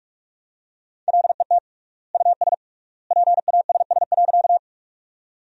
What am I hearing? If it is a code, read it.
Morse code, heard as a keyed tone, about 46 wpm, 700 Hz. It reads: LET US PASS.
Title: 3x Word-Spacing - Sentences from Top 400 Words - 45wpm